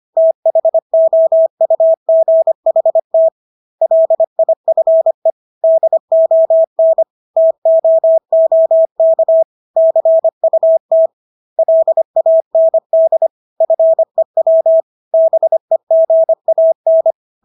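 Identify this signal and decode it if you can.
Morse code, heard as a keyed tone, about 25 wpm, 650 Hz. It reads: THOUGHT LIFE DON TOOK CUT LAND FEW BEGAN
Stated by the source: Morse practice of English words at 25 wpm from CW med Gnister.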